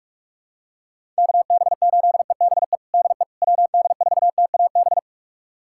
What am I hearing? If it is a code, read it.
Morse code, heard as a keyed tone, about 46 wpm, 700 Hz. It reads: KB8EBE DE WD4TAB
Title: USA Callsign DE Callsign - 6 Characters - 45wpm